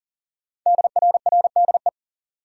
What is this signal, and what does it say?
Morse code, heard as a keyed tone, about 39 wpm, 700 Hz. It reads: DRRDE
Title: Code Groups - 5 Letters - 40wpm